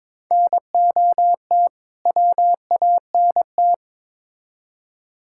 Morse code, heard as a keyed tone, about 22 wpm, 700 Hz. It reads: NOT WANT
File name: Build A Phrase - 2-Word Phrases - 22wpm